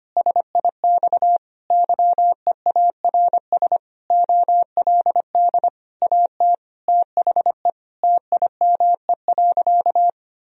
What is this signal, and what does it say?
Morse code, heard as a keyed tone, about 25 wpm, 700 Hz. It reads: SIX YEARS OLD AT THE TIME.
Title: Sentences from Top 1000 Words - Review Part 2 - 25wpm